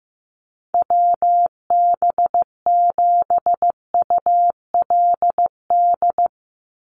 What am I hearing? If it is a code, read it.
Morse code, heard as a keyed tone, about 15 wpm, 700 Hz. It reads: WB7ULD
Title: NA SOTA Chaser Callsigns 15wpm